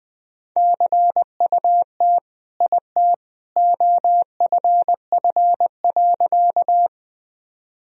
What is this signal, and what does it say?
Morse code, heard as a keyed tone, about 20 wpm, 700 Hz. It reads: CUT IT OFF.
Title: Sentences from Top 1000 Words - Review Part 1 - 20wpm